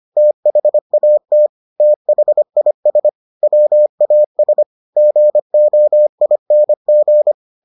Morse code, heard as a keyed tone, about 25 wpm, 600 Hz. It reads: THAT THIS WAS GOING